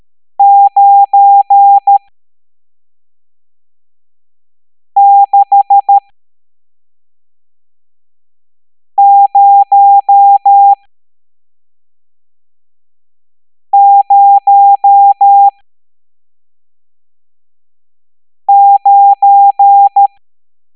Morse code, heard as a keyed tone, about 13 wpm, spaced as for 3 wpm, 800 Hz. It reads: 96009
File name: hamdata / L11-EX01-67890